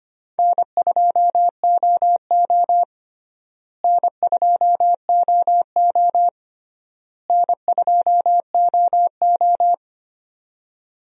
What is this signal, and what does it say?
Morse code, heard as a keyed tone, about 25 wpm, 700 Hz. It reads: N2OO N2OO N2OO
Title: USA Call Signs - 4 Characters - Repeated 3x - 25wpm